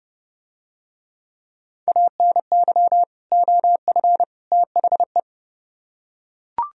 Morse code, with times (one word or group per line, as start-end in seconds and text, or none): 1.9-3.0 ANY
3.3-4.2 OF
4.5-5.2 THE